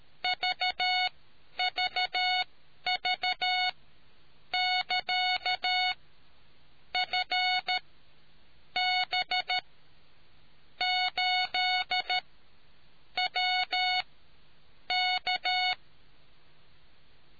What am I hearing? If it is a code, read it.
Morse code, heard as a keyed tone, about 13 wpm, spaced as for 7 wpm, 730 Hz. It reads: VVV<KA>FB8WK